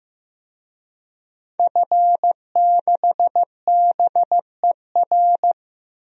Morse code, heard as a keyed tone, about 15 wpm, 700 Hz. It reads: F6BER
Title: DXCC-100 Call Signs - 5 Characters - 15wpm